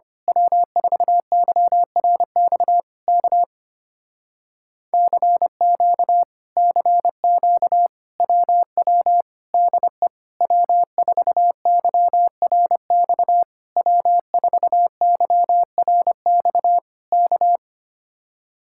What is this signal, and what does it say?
Morse code, heard as a keyed tone, about 24 wpm, 700 Hz. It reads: W4YRX K CQ CQ WW DE W4YRX W4YRX K
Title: Speed-Racing Common QSO Phrases Encore - 25wpm